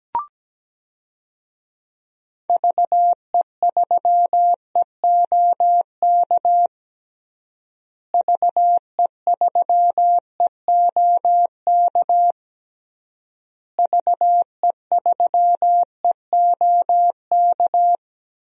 Morse code, heard as a keyed tone, about 17 wpm, 700 Hz. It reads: VE3EOK VE3EOK VE3EOK
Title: DXCC-100 Call Signs - 6 Characters - Repeated 3x - 17wpm